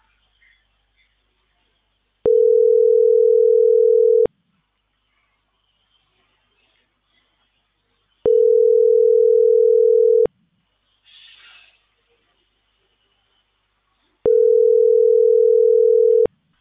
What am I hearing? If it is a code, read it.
Ringback tone (North America), 440 + 480 Hz.